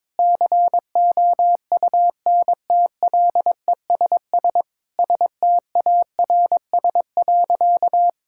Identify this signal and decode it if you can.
Morse code, heard as a keyed tone, about 22 wpm, 700 Hz. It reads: COUNTLESS STARS.
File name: Difficult Sentences with Hints - 22wpm